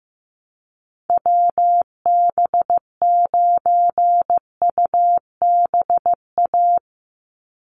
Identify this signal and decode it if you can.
Morse code, heard as a keyed tone, about 15 wpm, 700 Hz. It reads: WB9UBA
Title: USA Call Signs 15wpm